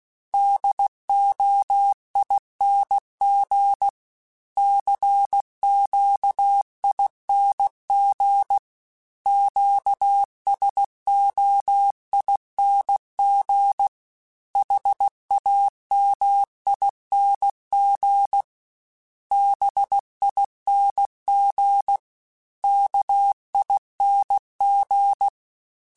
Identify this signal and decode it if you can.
Morse code, heard as a keyed tone, about 16 wpm, 800 Hz. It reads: DOING CQING QSOING HAMING BING KING
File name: Special INGs / Special INGs-mp3s / Sngl ing @15